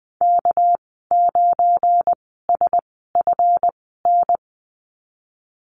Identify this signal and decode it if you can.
Morse code, heard as a keyed tone, about 20 wpm, spaced as for 15 wpm, 700 Hz. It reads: K9SFN
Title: 207 Lesson 33g - Letter j - Call Signs 20wpm